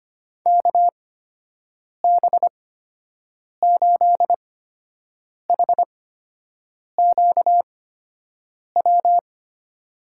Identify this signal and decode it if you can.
Morse code, heard as a keyed tone, about 25 wpm, spaced as for 7 wpm, 700 Hz. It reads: KB8HQW
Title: USA Call Signs - 6 Characters - 8x Character Spacing - 25wpm